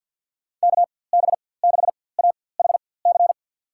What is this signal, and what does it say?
Morse code, heard as a keyed tone, about 50 wpm, spaced as for 23 wpm, 700 Hz. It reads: KB6AHC
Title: USA Call Signs - 6 Characters - 4x Character Spacing - 50wpm